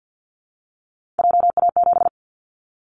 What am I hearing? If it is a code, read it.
Morse code, heard as a keyed tone, about 48 wpm, 700 Hz. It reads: JA7
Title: Callsign Prefixes for Top 25 DXCC - Repeated 3x - 50wpm